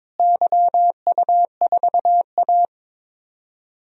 Morse code, heard as a keyed tone, about 22 wpm, 700 Hz. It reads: YU4A
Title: DXCC-100 Call Signs - 4 Characters - Repeated 3x - 22wpm